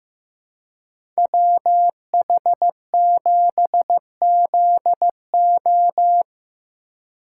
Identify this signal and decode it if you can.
Morse code, heard as a keyed tone, about 18 wpm, 700 Hz. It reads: WH7ZO
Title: Speed Racing - USA 2x2 Call Signs 15wpm